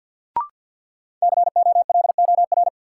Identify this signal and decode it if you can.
Morse code, heard as a keyed tone, about 53 wpm, 700 Hz. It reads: COLOR
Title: Sentences from Top 1000 Words - Review Part 2 - New Word Drill - 50wpm